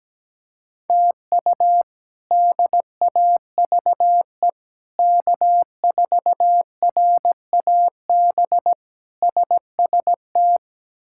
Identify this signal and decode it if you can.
Morse code, heard as a keyed tone, about 17 wpm, 700 Hz. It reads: TU DAVE K4RAB SST
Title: K1USN SST QSO Practice - 17wpm